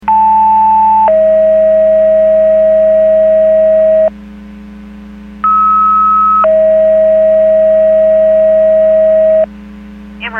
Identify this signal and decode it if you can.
Two-tone page, heard as a one-second tone, then a three-second tone, about 880 then 640 Hz, about 1290 then 640 Hz. It is heard from 0.1 to 9.4 s.
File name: Portage 154.385 Paging